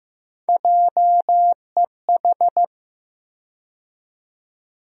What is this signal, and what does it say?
Morse code, heard as a keyed tone, about 15 wpm, 700 Hz. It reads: JEH
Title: USA Call Sign Postfixes - 3 Letters 15wpm